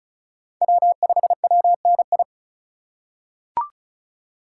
Morse code, heard as a keyed tone, about 34 wpm, 700 Hz. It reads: W5WNI
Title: USA Call Signs - 5 Characters - 8x Character Spacing - 35wpm